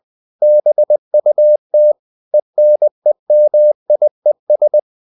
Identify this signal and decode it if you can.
Morse code, heard as a keyed tone, about 20 wpm, 600 Hz. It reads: BUT ENEMIES
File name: CW / Practice / EnglishQuotations0320